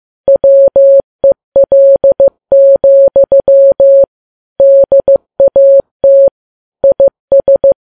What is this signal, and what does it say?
Morse code, heard as a keyed tone, about 15 wpm, 550 Hz. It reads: WEL, DAT IS
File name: Files Web tls / Morse on9fv / morse15kps